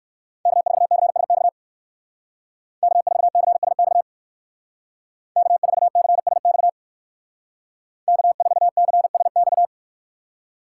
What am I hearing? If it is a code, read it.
Morse code, heard as a keyed tone, about 56 wpm, 700 Hz. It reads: K4CSX K4CSX K4CSX K4CSX
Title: Speed Racing - USA 1x3 Call Signs 45wpm